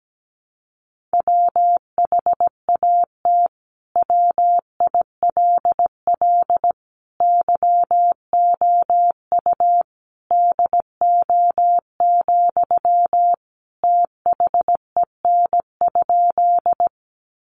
Morse code, heard as a keyed tone, about 17 wpm, 700 Hz. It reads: WHAT WILL YOU DO, THEN?
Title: Sentences from Top 100 Words 17wpm